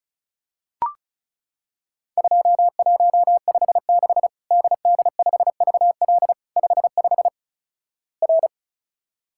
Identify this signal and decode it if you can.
Morse code, heard as a keyed tone, about 34 wpm, 700 Hz. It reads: 2156 DD5VL 55 R
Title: Worked All Europe DX Contest - QTC Exchange - 35wpm